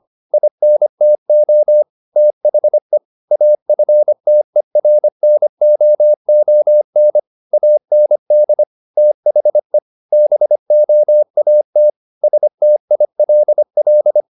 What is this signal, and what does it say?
Morse code, heard as a keyed tone, about 25 wpm, 600 Hz. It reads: INTO THE AFTERNOON AND THE BOAT STILL
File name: the old man and the sea part 1 25wpm